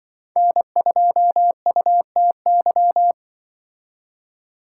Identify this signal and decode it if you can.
Morse code, heard as a keyed tone, about 24 wpm, 700 Hz. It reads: N2UTY
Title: Speed Racing - USA 1x3 Call Signs 20wpm